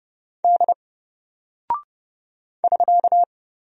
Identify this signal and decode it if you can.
Morse code, heard as a keyed tone, about 30 wpm, 700 Hz. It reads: D <SK>
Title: Single Character (Letter:Number:Punctuation:Prosign) 30wpm